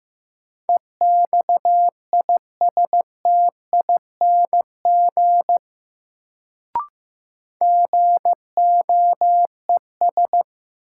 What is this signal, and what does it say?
Morse code, heard as a keyed tone, about 16 wpm, 700 Hz. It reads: EXISTING GOES
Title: Sentences from Top 1500 Words - New Word Drill Alt - 15wpm